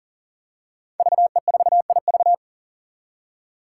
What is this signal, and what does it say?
Morse code, heard as a keyed tone, about 40 wpm, 700 Hz. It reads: VE4IV
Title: Five Challenging Letter-Number Combinations 40wpm